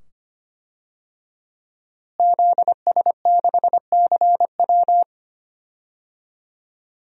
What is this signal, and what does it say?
Morse code, heard as a keyed tone, about 25 wpm, 700 Hz. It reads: ZS6CW
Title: DXCC-100 Call Signs - 5 Characters - Repeated 2x - 25wpm